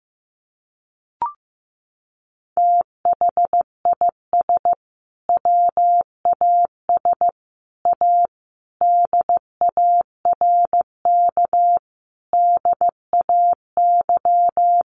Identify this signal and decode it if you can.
Morse code, heard as a keyed tone, about 15 wpm, 700 Hz. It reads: THIS WAS A DARK DAY
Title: Sentences from Top 1300 Words 15wpm